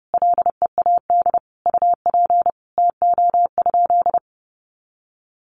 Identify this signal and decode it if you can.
Morse code, heard as a keyed tone, about 30 wpm, 700 Hz. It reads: LEAD UP TO?
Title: Sentences from Top 500 Words 30wpm